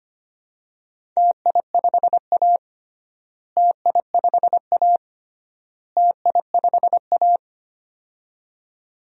Morse code, heard as a keyed tone, about 25 wpm, 700 Hz. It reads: TI5A TI5A TI5A
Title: DXCC-100 Call Signs - 4 Characters - Repeated 3x - 25wpm